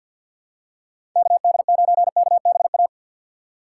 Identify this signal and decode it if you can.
Morse code, heard as a keyed tone, about 49 wpm, 700 Hz. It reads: KD9KBA
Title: USA Call Signs - 6 Characters - 3x Character Spacing - 50wpm